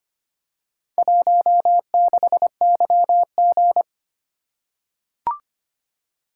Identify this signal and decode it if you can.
Morse code, heard as a keyed tone, about 25 wpm, 700 Hz. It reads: 16YG
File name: practice / 25wpm / Four Letter-Number Combinations - 25wpm